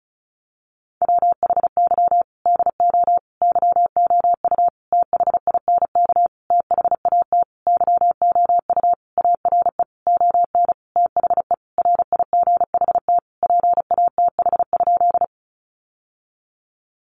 Morse code, heard as a keyed tone, about 34 wpm, 700 Hz. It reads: WHY DO YOU THINK THAT YOU ARE ON THE RIGHT PATH?
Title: Sentences from Top 1400 Words 35wpm